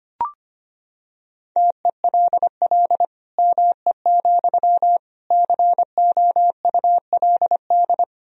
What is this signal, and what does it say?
Morse code, heard as a keyed tone, about 25 wpm, 700 Hz. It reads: TELL ME, COULD
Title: Sentences from Top 1000 Words - Review Part 2 - 25wpm